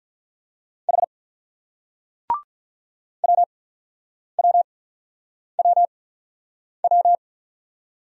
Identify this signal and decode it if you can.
Morse code, heard as a keyed tone, about 42 wpm, spaced as for 9 wpm, 700 Hz. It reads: H WWWW
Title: Speed-Racing Single Letters 35wpm